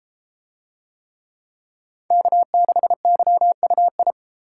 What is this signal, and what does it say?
Morse code, heard as a keyed tone, about 31 wpm, 700 Hz. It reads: K6YUI